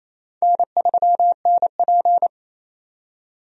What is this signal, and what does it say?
Morse code, heard as a keyed tone, about 28 wpm, 700 Hz. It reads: N3NP
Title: USA Call Signs - 4 Characters - 28wpm